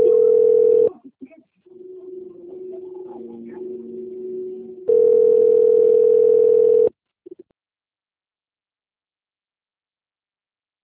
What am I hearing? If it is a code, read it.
Ringback tone (North America), 440 + 480 Hz.